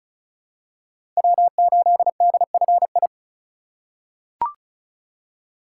Morse code, heard as a keyed tone, about 36 wpm, 690 Hz. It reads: W8DFI
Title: USA Call Signs - 3x Character Spacing - 35wpm